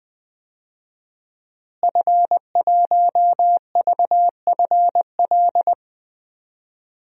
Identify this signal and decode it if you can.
Morse code, heard as a keyed tone, about 19 wpm, 700 Hz. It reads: F1VFL
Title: DXCC-100 Call Signs - 5 Characters - Repeated 3x - 20wpm